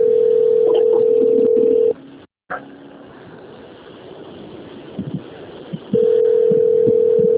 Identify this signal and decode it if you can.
Ringback tone (North America), 440 + 480 Hz.
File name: out- - - - - .6274